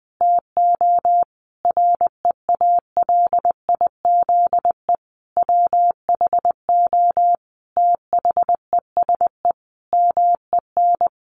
Morse code, heard as a keyed tone, about 19 wpm, 700 Hz. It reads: TO REALIZE WHO THESE MEN
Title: Sentences from Top 1000 Words 20wpm